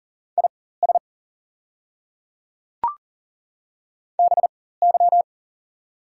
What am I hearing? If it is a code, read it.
Morse code, heard as a keyed tone, about 40 wpm, spaced as for 19 wpm, 690 Hz. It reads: IS BY
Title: 4x Character Spacing - Top 100 Words - 40wpm